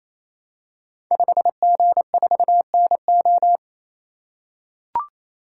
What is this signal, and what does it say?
Morse code, heard as a keyed tone, about 28 wpm, 700 Hz. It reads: 5G4NO
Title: Code Groups - 5 Letter:Numbers - 28wpm